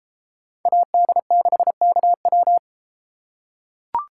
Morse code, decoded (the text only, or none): AD6KW